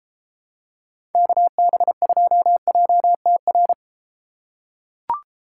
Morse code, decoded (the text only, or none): KB2JTR